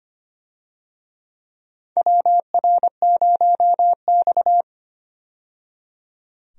Morse code, heard as a keyed tone, about 25 wpm, 700 Hz. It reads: WR0X
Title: USA Call Signs - 4 Characters - 25wpm